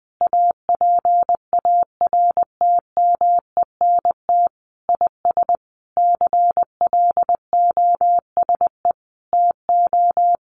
Morse code, heard as a keyed tone, about 20 wpm, 700 Hz. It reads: APARTMENT IS CLOSE TO
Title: Sentences from Top 900 Words 20wpm